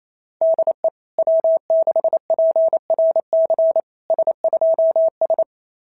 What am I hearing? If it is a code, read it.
Morse code, heard as a keyed tone, about 29 wpm, 660 Hz. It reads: DE W6PRC S2S
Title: SOTA QSO Practice 28wpm